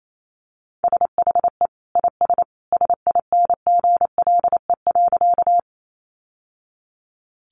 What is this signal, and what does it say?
Morse code, heard as a keyed tone, about 28 wpm, 700 Hz. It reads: SHE IS SINGLE.
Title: Sentences from Top 700 Words 28wpm